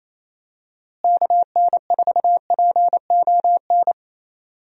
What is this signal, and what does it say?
Morse code, heard as a keyed tone, about 28 wpm, 700 Hz. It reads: KN4PON
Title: USA Call Signs - 6 Characters - Repeated 3x - 28wpm